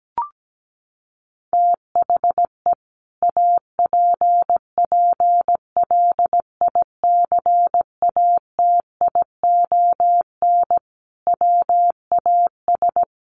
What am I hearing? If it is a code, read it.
Morse code, heard as a keyed tone, about 17 wpm, 700 Hz. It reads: THE APPLICATION WAS
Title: Sentences from Top 1200 Words 17wpm